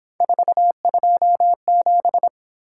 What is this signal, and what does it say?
Morse code, heard as a keyed tone, about 26 wpm, 700 Hz. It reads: 427
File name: Speed-Racing - Three Letter-Number Combinations - 22wpm